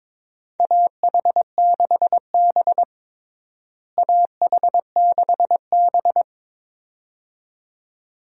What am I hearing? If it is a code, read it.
Morse code, heard as a keyed tone, about 22 wpm, 700 Hz. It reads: AH6B AH6B
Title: USA Call Signs - 4 Characters - Repeated 2x - 22wpm